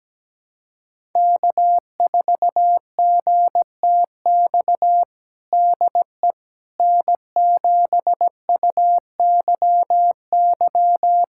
Morse code, heard as a keyed tone, about 17 wpm, 700 Hz. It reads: K4GTX DE N7UYY
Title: USA Callsign DE Callsign 17wpm